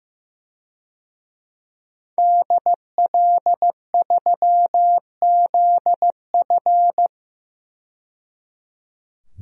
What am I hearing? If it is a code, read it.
Morse code, heard as a keyed tone, about 15 wpm, 700 Hz. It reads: DL3ZF
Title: DXCC-100 Call Signs - 5 Characters - 15wpm